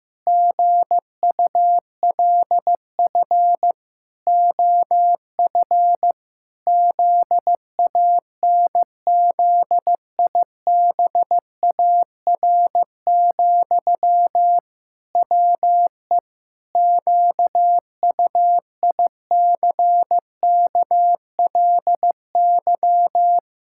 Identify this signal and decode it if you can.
Morse code, heard as a keyed tone, about 15 wpm, 700 Hz. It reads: GULF OF ZANZIBAR, WE QUICKLY
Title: Pangram Challenge v2 - 15wpm